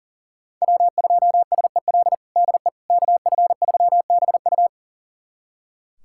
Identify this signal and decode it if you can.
Morse code, heard as a keyed tone, about 40 wpm, 700 Hz. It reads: W2SEL DE KF3BU
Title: USA Callsign DE Callsign - 5 Characters - 40wpm